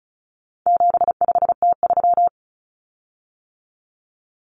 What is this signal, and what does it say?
Morse code, heard as a keyed tone, about 34 wpm, 700 Hz. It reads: 75T3
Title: Common Cut Numbers - 4-Digit 35wpm